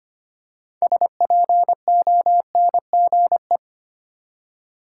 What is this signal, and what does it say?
Morse code, heard as a keyed tone, about 25 wpm, 690 Hz. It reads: SPONGE
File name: practice / 25wpm / Easy Words with Hints - 25wpm